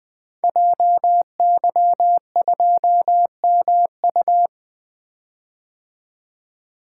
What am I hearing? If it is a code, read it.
Morse code, heard as a keyed tone, about 20 wpm, 700 Hz. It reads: JY2MU
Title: Code Groups - 5 Letter:Numbers - 20wpm